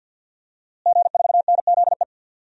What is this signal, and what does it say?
Morse code, heard as a keyed tone, about 50 wpm, 700 Hz. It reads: G4NZE
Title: DXCC-100 Call Signs - 5 Characters - Repeated 2x - 50wpm